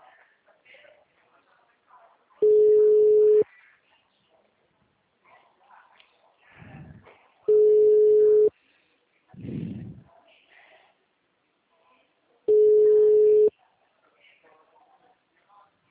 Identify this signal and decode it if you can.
Ringback tone (Europe, 425 Hz), with speech.